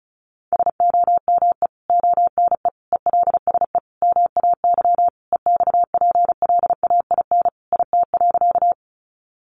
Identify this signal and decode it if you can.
Morse code, heard as a keyed tone, about 34 wpm, 700 Hz. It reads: SOME ONE ELSE MAY EXPLAIN IT.
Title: Sentences from Top 1100 Words 35wpm